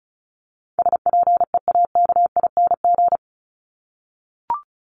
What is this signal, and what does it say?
Morse code, heard as a keyed tone, about 34 wpm, 700 Hz. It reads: SPEAKING
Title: ING Suffix - Top 100 Words with Base Words 35wpm